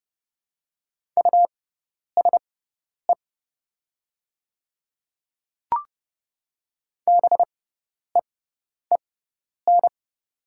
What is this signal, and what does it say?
Morse code, heard as a keyed tone, about 30 wpm, spaced as for 10 wpm, 700 Hz. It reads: USE BEEN
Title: 6x Character Spacing - Top 200 Words - 30wpm